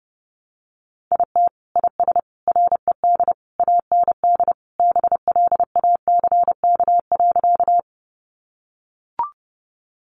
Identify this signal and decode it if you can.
Morse code, heard as a keyed tone, about 30 wpm, 700 Hz. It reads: IT IS RED AND BLACK.
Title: Easy Phrases 30wpm